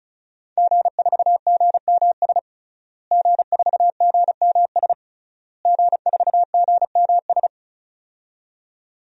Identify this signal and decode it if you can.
Morse code, heard as a keyed tone, about 34 wpm, 700 Hz. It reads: G4GMS G4GMS G4GMS
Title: DXCC-100 Call Signs - 5 Characters - Repeated 3x - 35wpm